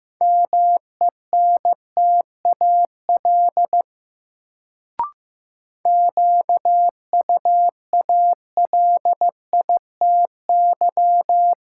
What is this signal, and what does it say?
Morse code, heard as a keyed tone, about 15 wpm, 700 Hz. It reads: MENTAL QUALITY
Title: Sentences from Top 1000 Words - Review Part 4 - New Word Drill - 15wpm